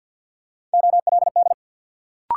Morse code, heard as a keyed tone, about 47 wpm, 700 Hz. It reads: OLD
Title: Sentences from Top 200 Words - New Word Drill - 50wpm